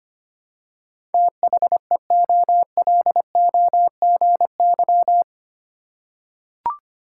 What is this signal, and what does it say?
Morse code, heard as a keyed tone, about 25 wpm, 700 Hz. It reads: THEOLOGY
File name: 8 Letter Words - First 3 Letters Given - 25wpm